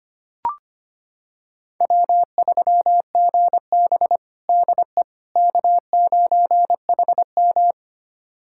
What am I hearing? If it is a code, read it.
Morse code, heard as a keyed tone, about 25 wpm, 700 Hz. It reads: W3GB DE K9HM
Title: USA Callsign DE Callsign - 4 Characters - 25wpm